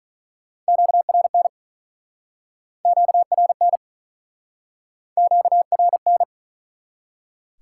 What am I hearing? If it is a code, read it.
Morse code, heard as a keyed tone, about 43 wpm, 700 Hz. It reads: QRN QRN QRN
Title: Speed-Racing Top 90 QSO Elements 35wpm